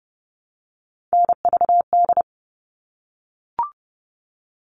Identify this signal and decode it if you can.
Morse code, heard as a keyed tone, about 30 wpm, 700 Hz. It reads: NVD
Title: All Cut Numbers - 3-Digit 30wpm